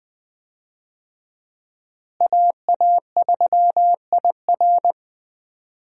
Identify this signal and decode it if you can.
Morse code, heard as a keyed tone, about 20 wpm, 700 Hz. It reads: AA3IR
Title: USA Call Signs - 5 Characters - Repeated 3x - 20wpm